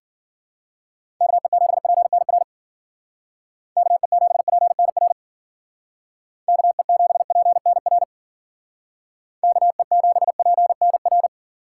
Morse code, heard as a keyed tone, about 52 wpm, 700 Hz. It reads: KE7PNR KE7PNR KE7PNR KE7PNR